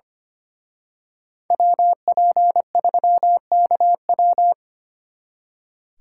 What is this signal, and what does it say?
Morse code, heard as a keyed tone, about 29 wpm, 700 Hz. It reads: WP3KW
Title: Speed Racing - USA 2x2 Call Signs 25wpm